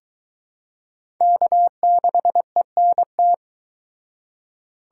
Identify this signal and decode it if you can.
Morse code, heard as a keyed tone, about 21 wpm, 700 Hz. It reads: K6ENT